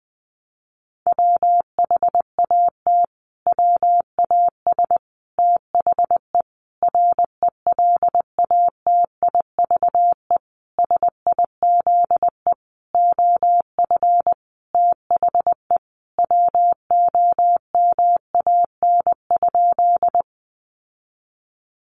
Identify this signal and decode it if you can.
Morse code, heard as a keyed tone, about 20 wpm, 700 Hz. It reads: WHAT WAS THE RELATIVE SIZE OF THE WOMAN?
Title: Sentences from Top 1400 Words 20wpm